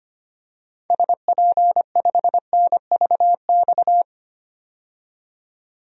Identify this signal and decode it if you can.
Morse code, heard as a keyed tone, about 25 wpm, 700 Hz. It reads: SP5NVX
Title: DXCC-100 Call Signs - 6 Characters - 25wpm